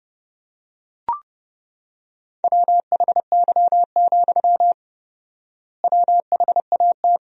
Morse code, heard as a keyed tone, about 32 wpm, 700 Hz. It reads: WHY, WHAT
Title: 4x Word-Spacing - Sentences from Top 200 Words - 30wpm